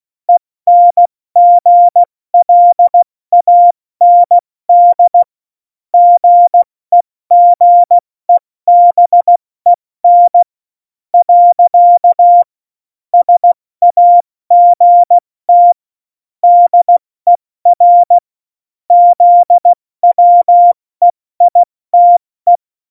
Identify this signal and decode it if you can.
Morse code, heard as a keyed tone, about 16 wpm, spaced as for 14 wpm, 700 Hz. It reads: ENGLAND GEGEBEN. SAGT DER ZWEITE
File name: Morsekurs / Witze / Witz 01. Dezember 2020, 70 F80